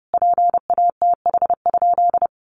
Morse code, heard as a keyed tone, about 30 wpm, 700 Hz. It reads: PATH?
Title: Sentences from Top 1400 Words 30wpm